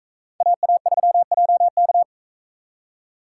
Morse code, heard as a keyed tone, about 42 wpm, 700 Hz. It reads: AA3JK